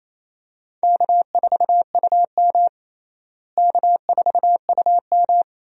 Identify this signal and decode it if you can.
Morse code, heard as a keyed tone, about 28 wpm, 700 Hz. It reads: K4UM K4UM
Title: USA Call Signs - 4 Characters - Repeated 2x - 28wpm